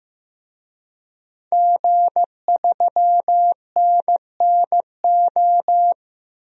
Morse code, heard as a keyed tone, about 15 wpm, 700 Hz. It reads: G3NNO
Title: DXCC-100 Call Signs - 5 Characters - 15wpm